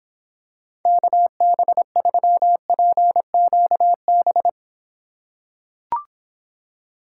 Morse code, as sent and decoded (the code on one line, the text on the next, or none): -.- -... ...-- .--. --.- -...
KB3PQB